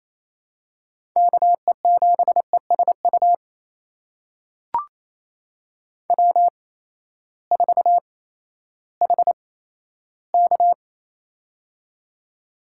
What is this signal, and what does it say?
Morse code, heard as a keyed tone, about 28 wpm, spaced as for 14 wpm, 700 Hz. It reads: KE7ESU W4HK
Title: USA Call Signs - 8x Character Spacing - 28wpm